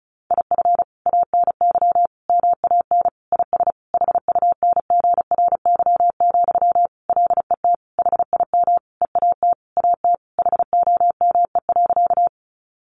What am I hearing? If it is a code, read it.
Morse code, heard as a keyed tone, about 34 wpm, 700 Hz. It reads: IF ANY MAN IS HUNGRY, LET HIM EAT AT HOME.
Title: Sentences from Top 1400 Words 35wpm